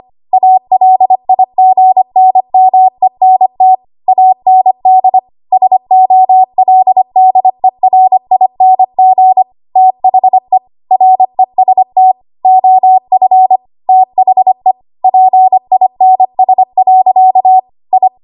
Morse code, heard as a keyed tone, about 25 wpm, 750 Hz. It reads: ALIGNMENT AND SOLDERING THE REST OF THE PINS. I
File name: Archive / 25 WPM / 25WPM